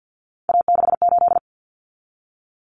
Morse code, heard as a keyed tone, about 52 wpm, 700 Hz. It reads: A68
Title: Common Cut Numbers - 3-Digit 50wpm